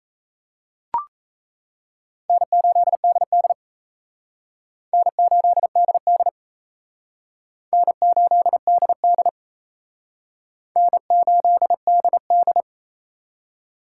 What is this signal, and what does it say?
Morse code, heard as a keyed tone, about 34 wpm, 700 Hz. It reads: N8DD N8DD N8DD N8DD
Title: Speed Racing - USA Call Signs 28wpm